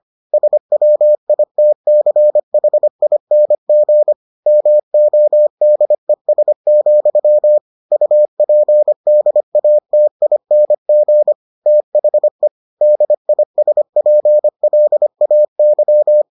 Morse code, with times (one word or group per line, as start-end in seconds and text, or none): 0.3-4.1 SWITCHING
4.5-7.6 MODES,
7.9-11.3 UPDATING
11.7-12.5 THE
12.8-16.2 DISPLAY